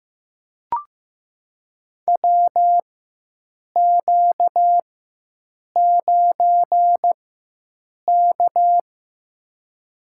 Morse code, heard as a keyed tone, about 15 wpm, spaced as for 7 wpm, 700 Hz. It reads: WQ9K
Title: USA Call Signs - 4 Characters - 4x Character Spacing - 15wpm